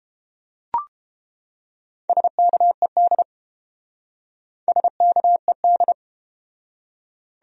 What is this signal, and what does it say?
Morse code, heard as a keyed tone, about 30 wpm, 700 Hz. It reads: SKED SKED